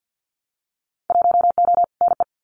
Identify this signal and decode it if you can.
Morse code, heard as a keyed tone, about 50 wpm, 700 Hz. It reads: 1O NE
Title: Winter Field Day Exchanges Encore 50wpm